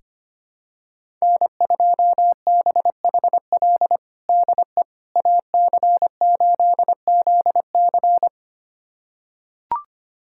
Morse code, heard as a keyed tone, about 25 wpm, 700 Hz. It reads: N2BHL DE AC8ZC
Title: USA Callsign DE Callsign - 5 Characters - 25wpm